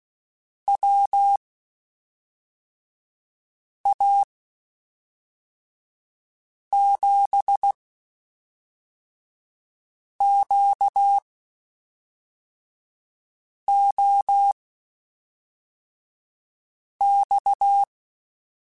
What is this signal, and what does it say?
Morse code, heard as a keyed tone, about 16 wpm, spaced as for 3 wpm, 800 Hz. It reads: WA7QOX
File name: (spaced) / QSOs Spaced-mp3s / WA7QOX 1 @5 sp